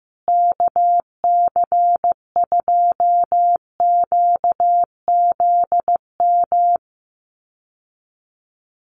Morse code, heard as a keyed tone, about 15 wpm, 700 Hz. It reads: KC2QZM
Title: USA Call Signs Repeated 2x 15wpm